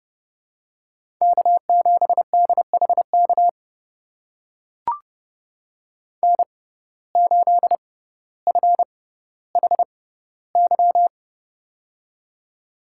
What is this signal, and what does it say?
Morse code, heard as a keyed tone, about 31 wpm, spaced as for 16 wpm, 700 Hz. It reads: K7DHK N8FHY
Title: USA Call Signs - 5 Characters - 6x Character Spacing - 30wpm